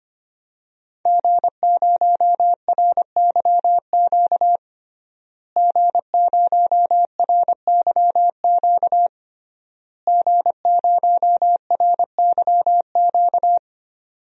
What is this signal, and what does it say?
Morse code, heard as a keyed tone, about 25 wpm, 700 Hz. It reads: G0RYQ G0RYQ G0RYQ
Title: DXCC-100 Call Signs - 5 Characters - Repeated 3x - 25wpm